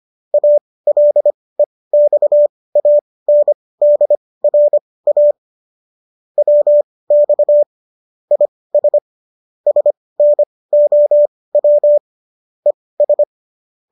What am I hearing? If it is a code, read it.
Morse code, heard as a keyed tone, about 25 wpm, spaced as for 18 wpm, 590 Hz. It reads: ALEXANDRA WX IS SNOW ES